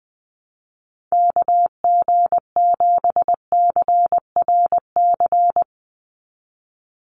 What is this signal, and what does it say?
Morse code, heard as a keyed tone, about 19 wpm, 700 Hz. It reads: KG7CRC
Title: USA Call Signs 20wpm